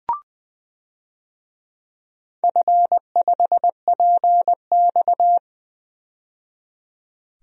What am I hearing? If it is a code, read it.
Morse code, heard as a keyed tone, about 20 wpm, 700 Hz. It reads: F5PX
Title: DXCC-100 Call Signs - 4 Characters - 20wpm